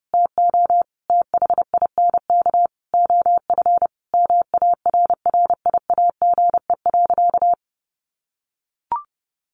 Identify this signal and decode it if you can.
Morse code, heard as a keyed tone, about 28 wpm, 700 Hz. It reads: TO THINK OF MARRIAGE.
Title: Sentences from Top 800 Words 30wpm